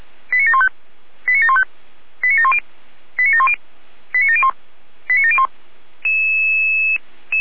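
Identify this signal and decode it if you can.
Five-tone selective call (ZVEI), heard as quick runs of five tones, tones of about 70 ms. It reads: 89715 89715 89710 89710 89701 89701
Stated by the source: Received by SDR on a ZVEI selective-call channel.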